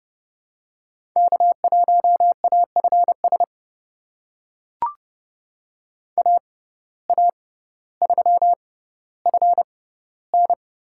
Morse code, heard as a keyed tone, about 30 wpm, spaced as for 15 wpm, 700 Hz. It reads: K1AFS AA3FN
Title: USA Call Signs - 5 Characters - 6x Character Spacing - 30wpm